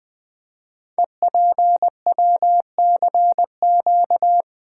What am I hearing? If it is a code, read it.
Morse code, heard as a keyed tone, about 20 wpm, 700 Hz. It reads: EPWCQ
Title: Code Groups - 5 Letter:Numbers - 20wpm